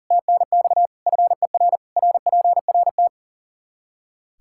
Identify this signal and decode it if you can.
Morse code, heard as a keyed tone, about 40 wpm, 700 Hz. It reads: TNX FER RPRT
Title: Practice QSO - Exchange 2 - Info - 40wpm